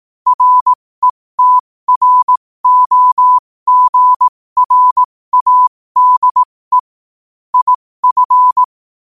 Morse code, heard as a keyed tone, about 18 wpm, spaced as for 15 wpm, 1000 Hz. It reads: RETROGRADE IF
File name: qotdfiles / 15wpm